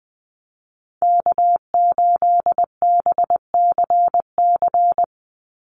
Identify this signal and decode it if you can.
Morse code, heard as a keyed tone, about 20 wpm, 700 Hz. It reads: K8BCC
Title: USA Call Signs 20wpm